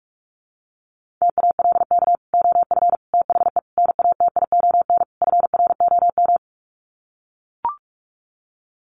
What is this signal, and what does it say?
Morse code, heard as a keyed tone, about 44 wpm, 700 Hz. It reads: TALK OF THE NATION FROM
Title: Sets of 5 Words 45wpm